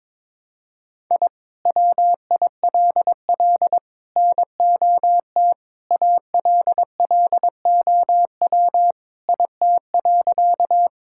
Morse code, heard as a keyed tone, about 22 wpm, 700 Hz. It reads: I WILL NOT ALLOW IT.
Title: Sentences from Top 1000 Words - Review Part 5 - 22wpm